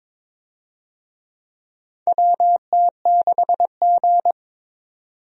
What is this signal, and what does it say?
Morse code, heard as a keyed tone, about 22 wpm, 700 Hz. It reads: WT6G
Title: USA Call Signs - 4 Characters - 22wpm